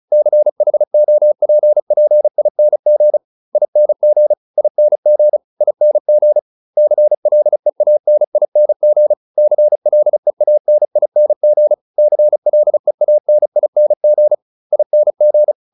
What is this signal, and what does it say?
Morse code, heard as a keyed tone, about 34 wpm, 600 Hz. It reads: CHOPPING ING ING ING CLEANING CLEANING CLEANING ING